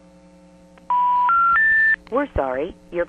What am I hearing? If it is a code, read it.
Special information tone, with speech.
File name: TurtleCreekEWSD